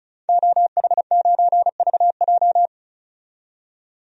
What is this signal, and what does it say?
Morse code, heard as a keyed tone, about 34 wpm, 700 Hz. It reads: OH9VJ
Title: DXCC-100 Call Signs - 5 Characters - 35wpm